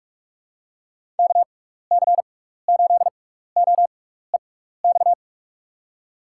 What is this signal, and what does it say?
Morse code, heard as a keyed tone, about 44 wpm, spaced as for 15 wpm, 700 Hz. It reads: KC8OEX
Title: USA Call Signs - 6 Characters - 6x Character Spacing - 45wpm